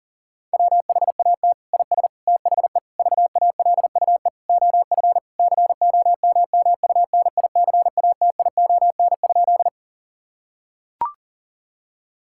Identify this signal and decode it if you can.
Morse code, heard as a keyed tone, about 40 wpm, 700 Hz. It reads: WHAT IS THE VALUE OF COMMUNICATION?
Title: Sentences from Top 1000 Words - Review Part 6 - 40wpm